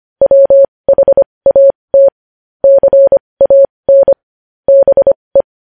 Morse code, heard as a keyed tone, about 25 wpm, spaced as for 20 wpm, 550 Hz. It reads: WHAT CAN BE
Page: news headlines in morse code at 20 wpm